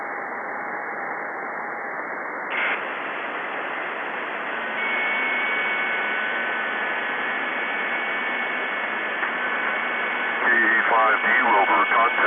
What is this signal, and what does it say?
FT8, with speech.